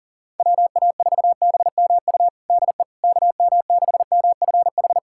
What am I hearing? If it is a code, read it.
Morse code, heard as a keyed tone, about 40 wpm, 700 Hz. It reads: WA4BMU DE KM6MFH
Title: USA Callsign DE Callsign 40wpm